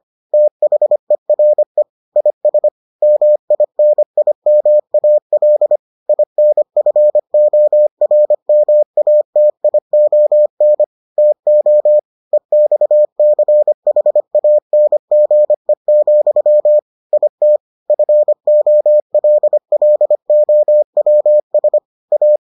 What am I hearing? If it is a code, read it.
Morse code, heard as a keyed tone, about 25 wpm, 600 Hz. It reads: THERE IS MINIMAL INFORMATION TO EXCHANGE, IT FOLLOWS A